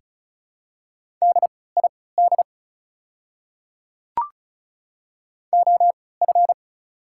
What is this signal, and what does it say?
Morse code, heard as a keyed tone, about 34 wpm, spaced as for 20 wpm, 690 Hz. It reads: DID OF
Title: 3x Character Spacing - Top 100 Words - 35wpm